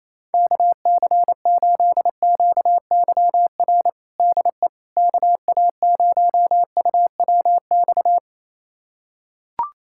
Morse code, heard as a keyed tone, about 28 wpm, 700 Hz. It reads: KC8QYR DE KA0UWX
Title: USA Callsign DE Callsign - 6 Characters - 28wpm